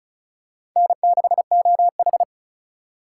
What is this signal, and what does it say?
Morse code, heard as a keyed tone, about 36 wpm, 700 Hz. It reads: N6OH
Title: USA Call Signs - 4 Characters - 4x Character Spacing - 35wpm